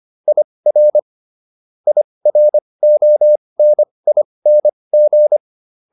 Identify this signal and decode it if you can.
Morse code, heard as a keyed tone, about 25 wpm, spaced as for 20 wpm, 600 Hz. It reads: IR IRONING